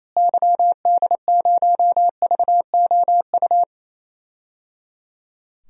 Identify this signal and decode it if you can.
Morse code, heard as a keyed tone, about 28 wpm, 700 Hz. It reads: YD0VOU
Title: DXCC-100 Call Signs - 6 Characters - Repeated 3x - 28wpm